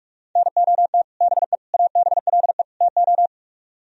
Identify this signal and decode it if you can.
Morse code, heard as a keyed tone, about 46 wpm, 700 Hz. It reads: NOT BE ABLE TO